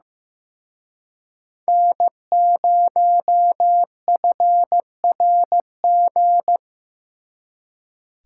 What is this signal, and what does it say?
Morse code, heard as a keyed tone, about 18 wpm, 700 Hz. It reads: N0FRG
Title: Speed Racing - USA 1x3 Call Signs 15wpm